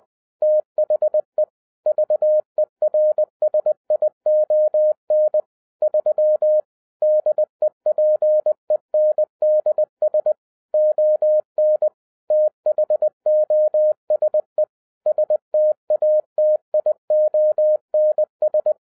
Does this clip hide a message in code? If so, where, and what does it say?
Yes: Morse code from 0.4 to 18.7 s, heard as a keyed tone, about 20 wpm, 610 Hz. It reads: THE VERSION 3 DEPENDS ON THOSE STATIONS